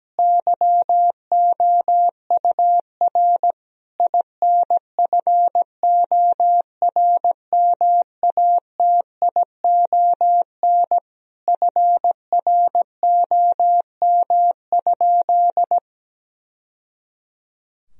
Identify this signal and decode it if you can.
Morse code, heard as a keyed tone, about 17 wpm, 700 Hz. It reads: YOUR INFORMATION FROM?
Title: Sentences from Top 1000 Words - Review Part 3 - 17wpm